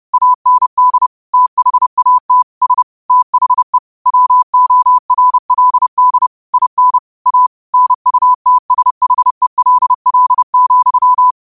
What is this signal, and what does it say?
Morse code, heard as a keyed tone, about 30 wpm, 1000 Hz. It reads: AND THAT S THE WORLD IN A NUTSHELL,
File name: 30wpm